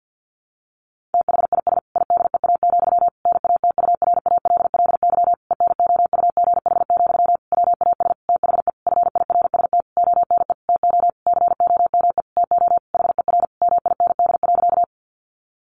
Morse code, heard as a keyed tone, about 50 wpm, 700 Hz. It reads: THIS IDEA, NATURALLY ENOUGH, WAS THE FIRST ONE TO COME TO HER MIND.
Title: Sentences from Top 1300 Words 50wpm